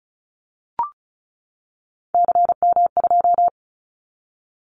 Morse code, heard as a keyed tone, about 36 wpm, 700 Hz. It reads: CM2
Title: Callsign Prefixes for Top 100 DXCC - 35wpm